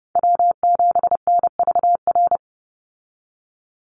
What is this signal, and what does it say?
Morse code, heard as a keyed tone, about 30 wpm, 700 Hz. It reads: W7NVR
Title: USA Call Signs 30wpm